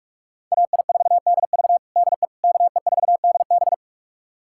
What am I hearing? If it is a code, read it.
Morse code, heard as a keyed tone, about 42 wpm, 700 Hz. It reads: AI4DV DE KE4DB
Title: USA Callsign DE Callsign - 5 Characters - 45wpm